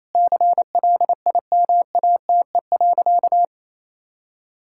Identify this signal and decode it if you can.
Morse code, heard as a keyed tone, about 28 wpm, 700 Hz. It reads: CLIMATE.
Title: Sentences from Top 1600 Words 28wpm